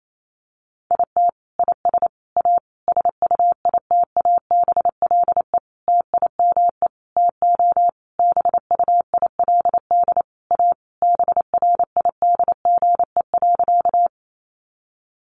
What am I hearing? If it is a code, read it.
Morse code, heard as a keyed tone, about 28 wpm, 700 Hz. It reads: IT IS A SUITABLE TIME TO BUILD A BRIDGE.
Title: Sentences from Top 1400 Words 28wpm